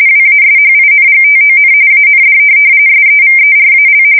RTTY, about 45.45 baud. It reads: WASHINGTON AND SEVERAL
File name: fsk170 / USS-Massachusetts